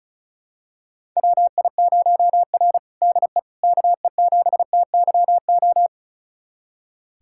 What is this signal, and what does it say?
Morse code, heard as a keyed tone, about 34 wpm, 700 Hz. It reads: WI0R DE KE7TYO
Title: USA Callsign DE Callsign 35wpm